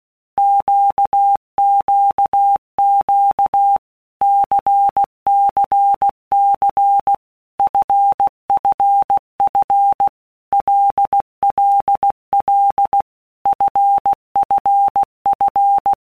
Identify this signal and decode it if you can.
Morse code, heard as a keyed tone, about 16 wpm, 800 Hz. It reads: QQQ CCC FFF LLL FFF
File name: Basic Letter Groups By Blocks / Groups@15 / LFCYQ@15